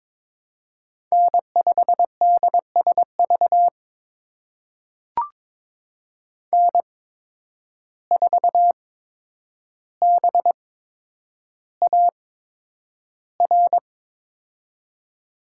Morse code, heard as a keyed tone, about 22 wpm, spaced as for 9 wpm, 700 Hz. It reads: N5DSV N4BAR
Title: USA Call Signs - 5 Characters - 8x Character Spacing - 22wpm